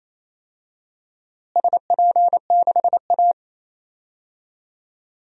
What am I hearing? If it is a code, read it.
Morse code, heard as a keyed tone, about 28 wpm, 700 Hz. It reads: SP6A
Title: DXCC-100 Call Signs - 4 Characters - Repeated 2x - 28wpm